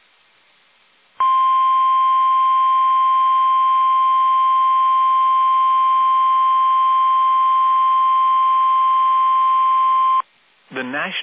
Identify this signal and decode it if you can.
Emergency Alert System (SAME) tones (the 1050 Hz attention tone), with speech. It is heard from 1.1 to 10.3 s.